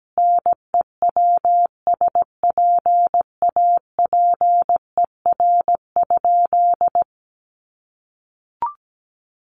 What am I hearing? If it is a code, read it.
Morse code, heard as a keyed tone, about 17 wpm, 700 Hz. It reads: NEWSPAPER?
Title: Sentences from Top 1100 Words 17wpm